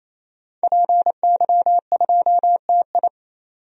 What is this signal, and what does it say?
Morse code, heard as a keyed tone, about 28 wpm, 700 Hz. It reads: PY2TI
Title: DXCC-100 Call Signs - 5 Characters - 28wpm